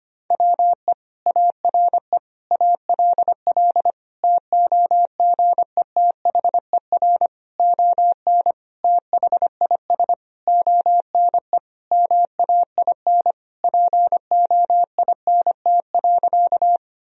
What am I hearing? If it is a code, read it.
Morse code, heard as a keyed tone, about 25 wpm, 700 Hz. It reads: WE ARE ALL TOGETHER ON THIS ONE MAIN POINT.
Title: Sentences from Top 1000 Words - Review Part 2 - 25wpm